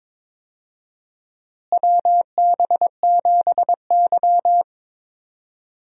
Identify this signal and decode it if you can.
Morse code, heard as a keyed tone, about 22 wpm, 700 Hz. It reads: WB7Y
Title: USA Call Signs - 4 Characters - Repeated 2x - 22wpm